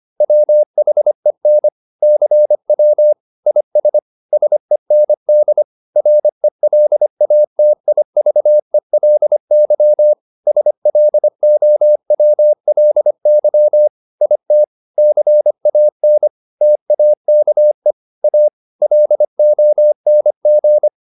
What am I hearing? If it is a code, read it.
Morse code, heard as a keyed tone, about 25 wpm, 600 Hz. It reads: WHEN CW IS SEND RELATIVELY SLOWLY IT CAN TAKE A LONG